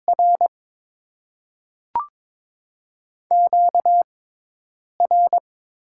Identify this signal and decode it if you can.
Morse code, heard as a keyed tone, about 22 wpm, 700 Hz. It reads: R Q R
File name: Top 90 QSO Elements - 6x Character Spacing - 22wpm